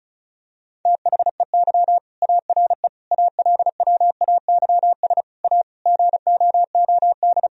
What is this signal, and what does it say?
Morse code, heard as a keyed tone, about 36 wpm, 700 Hz. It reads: THEY ARE ALWAYS A GOOD
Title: Sentences from Top 1000 Words - Review Part 4 - 35wpm